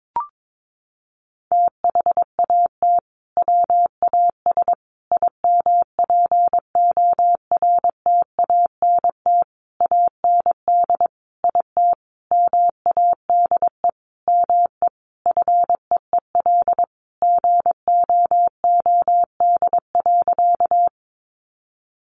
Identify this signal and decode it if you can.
Morse code, heard as a keyed tone, about 22 wpm, 700 Hz. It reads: THAT WAS IMPORTANT AND IT MADE ME FEEL GOOD.
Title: Sentences from Top 600 Words 22wpm